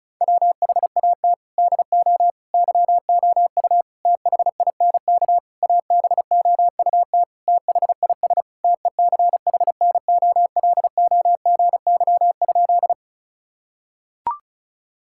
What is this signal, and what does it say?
Morse code, heard as a keyed tone, about 36 wpm, 700 Hz. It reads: WHAT DO YOU THINK ABOUT THIS TECHNOLOGY?
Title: Sentences from Top 1000 Words - Review Part 5 - 35wpm